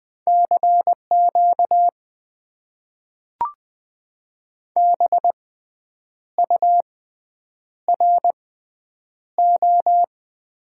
Morse code, heard as a keyed tone, about 20 wpm, spaced as for 8 wpm, 700 Hz. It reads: CQ BURO
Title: Top 90 QSO Elements - 6x Character Spacing - 20wpm